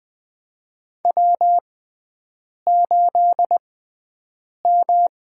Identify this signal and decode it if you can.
Morse code, heard as a keyed tone, about 20 wpm, spaced as for 7 wpm, 700 Hz. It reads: W8M
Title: USA Call Signs - 6x Character Spacing - 20wpm